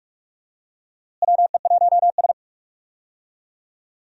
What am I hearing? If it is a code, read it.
Morse code, heard as a keyed tone, about 44 wpm, 700 Hz. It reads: WE1S
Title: USA Call Signs - 4 Characters - Repeated 2x - 45wpm